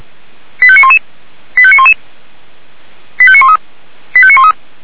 Five-tone selective call (ZVEI), heard as quick runs of five tones, tones of about 70 ms. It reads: 85911 85911 85913 85913